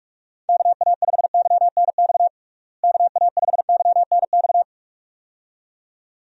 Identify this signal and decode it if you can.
Morse code, heard as a keyed tone, about 46 wpm, 700 Hz. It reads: KA5YNX KA5YNX
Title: USA Call Signs - 6 Characters - Repeated 2x - 45wpm